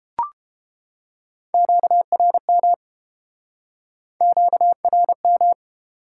Morse code, heard as a keyed tone, about 30 wpm, 700 Hz. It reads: QRM QRM